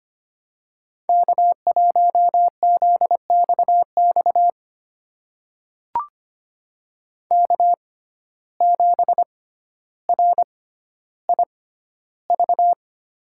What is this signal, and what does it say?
Morse code, heard as a keyed tone, about 25 wpm, spaced as for 13 wpm, 700 Hz. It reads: K1ZXX K7RIV
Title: USA Call Signs - 5 Characters - 6x Character Spacing - 25wpm